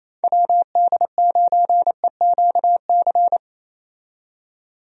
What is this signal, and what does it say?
Morse code, heard as a keyed tone, about 28 wpm, 700 Hz. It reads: WD9EQC
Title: USA Call Signs - 6 Characters - 28wpm